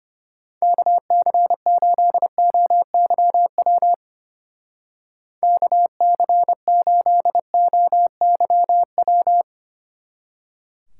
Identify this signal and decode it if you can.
Morse code, heard as a keyed tone, about 32 wpm, 700 Hz. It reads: KC8OYW KC8OYW